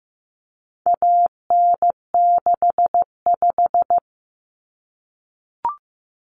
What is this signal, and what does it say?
Morse code, heard as a keyed tone, about 15 wpm, 700 Hz. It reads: AN65
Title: Common Cut Numbers - 4-Digit 15wpm